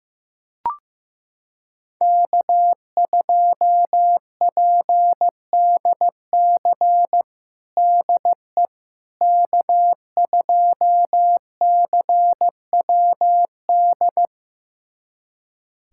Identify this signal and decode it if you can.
Morse code, heard as a keyed tone, about 15 wpm, 700 Hz. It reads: K2PDC DE K2CWD
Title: USA Callsign DE Callsign - 5 Characters - 15wpm